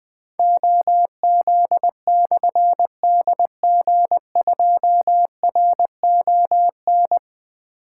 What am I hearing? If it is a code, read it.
Morse code, heard as a keyed tone, about 20 wpm, 700 Hz. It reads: OZ/DG2RON
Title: Difficult Call Signs 20wpm